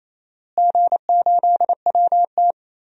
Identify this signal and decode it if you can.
Morse code, heard as a keyed tone, about 28 wpm, 700 Hz. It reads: G8WT
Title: DXCC-100 Call Signs - 4 Characters - Repeated 2x - 28wpm